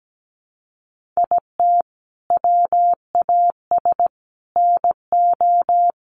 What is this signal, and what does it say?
Morse code, heard as a keyed tone, about 17 wpm, 700 Hz. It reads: IT WAS NO
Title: Sentences from Top 1500 Words 17wpm